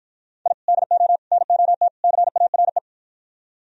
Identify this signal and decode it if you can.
Morse code, heard as a keyed tone, about 52 wpm, 700 Hz. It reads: I DO NOT CARE